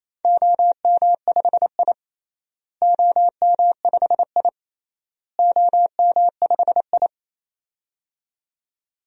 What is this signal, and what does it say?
Morse code, heard as a keyed tone, about 27 wpm, 700 Hz. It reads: OM5I OM5I OM5I
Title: DXCC-100 Call Signs - 4 Characters - Repeated 3x - 28wpm